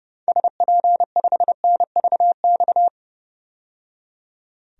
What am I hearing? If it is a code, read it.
Morse code, heard as a keyed tone, about 31 wpm, 690 Hz. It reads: SP5NVX
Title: DXCC-100 Call Signs - 6 Characters - 30wpm